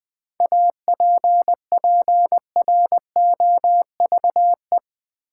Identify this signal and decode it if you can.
Morse code, heard as a keyed tone, about 20 wpm, 700 Hz. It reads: APPROVE
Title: Seven Letter Words - 20wpm